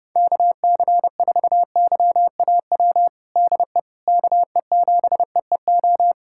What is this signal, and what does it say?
Morse code, heard as a keyed tone, about 30 wpm, 700 Hz. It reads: KC4YAW DE KE7EEO
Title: USA Callsign DE Callsign 30wpm